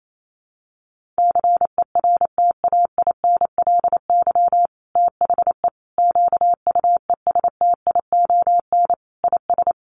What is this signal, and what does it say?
Morse code, heard as a keyed tone, about 28 wpm, 700 Hz. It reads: CERTAINLY THE QUESTION IS
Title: Sentences from Top 1200 Words 28wpm